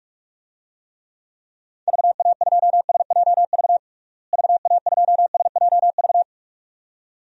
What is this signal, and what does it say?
Morse code, heard as a keyed tone, about 46 wpm, 700 Hz. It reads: VA2SJV VA2SJV
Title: DXCC-100 Call Signs - 6 Characters - Repeated 2x - 45wpm